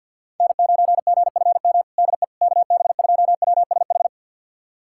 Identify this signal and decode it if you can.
Morse code, heard as a keyed tone, about 51 wpm, 700 Hz. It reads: N9GFM DE KB2WSH
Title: USA Callsign DE Callsign 50wpm